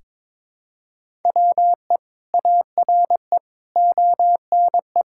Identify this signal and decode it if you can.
Morse code, heard as a keyed tone, about 23 wpm, 700 Hz. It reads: WE ARE ONE
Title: Sentences from Top 1000 Words - Review Part 6 - 22wpm